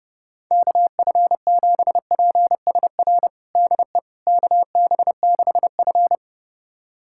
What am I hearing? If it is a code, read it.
Morse code, heard as a keyed tone, about 30 wpm, 700 Hz. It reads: KF7PSR DE KB6F
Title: USA Callsign DE Callsign 30wpm